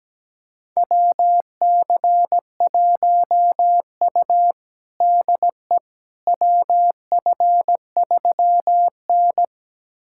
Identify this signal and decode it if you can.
Morse code, heard as a keyed tone, about 17 wpm, 700 Hz. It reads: WC1U DE WF3N
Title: USA Callsign DE Callsign - 4 Characters - 17wpm